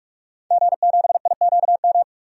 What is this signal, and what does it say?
Morse code, heard as a keyed tone, about 42 wpm, 700 Hz. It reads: G7IQM
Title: DXCC-100 Call Signs - 5 Characters - 45wpm